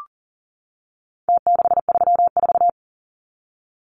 Morse code, heard as a keyed tone, about 40 wpm, 700 Hz. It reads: T634
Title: Numbers - Contest Style - 4-Digit 40wpm